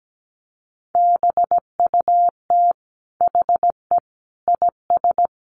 Morse code, heard as a keyed tone, about 17 wpm, 700 Hz. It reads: BUT HE IS
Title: Sentences from Top 1400 Words 17wpm